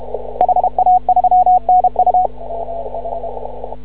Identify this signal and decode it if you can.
Morse code, heard as a keyed tone, about 33 wpm, 730 Hz. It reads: HA3NU
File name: CW / ha6oi 2